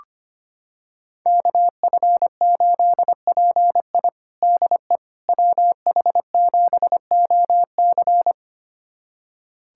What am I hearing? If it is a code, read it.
Morse code, heard as a keyed tone, about 25 wpm, 700 Hz. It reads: KF8PI DE WH7OC
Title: USA Callsign DE Callsign - 5 Characters - 25wpm